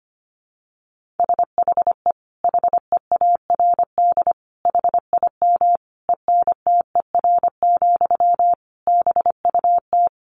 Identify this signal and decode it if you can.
Morse code, heard as a keyed tone, about 25 wpm, 700 Hz. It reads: SHE HEARD HIM ENTER, BUT
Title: Sentences from Top 1400 Words 25wpm